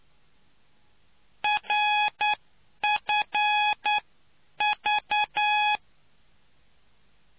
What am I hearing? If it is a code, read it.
Morse code, heard as a keyed tone, about 9 wpm, spaced as for 8 wpm, 840 Hz. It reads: RFV